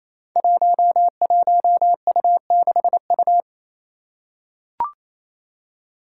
Morse code, heard as a keyed tone, about 28 wpm, 700 Hz. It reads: 11U6U
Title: Five Challenging Letter-Number Combinations 28wpm